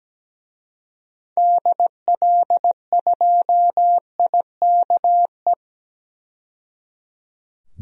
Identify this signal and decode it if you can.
Morse code, heard as a keyed tone, about 17 wpm, 700 Hz. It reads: DL2IKE
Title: DXCC-100 Call Signs - 6 Characters - 17wpm